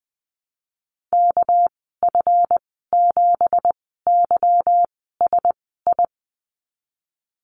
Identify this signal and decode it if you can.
Morse code, heard as a keyed tone, about 20 wpm, spaced as for 14 wpm, 700 Hz. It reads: KF7YSI
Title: 207 Lesson 33g - Letter j - Call Signs 20wpm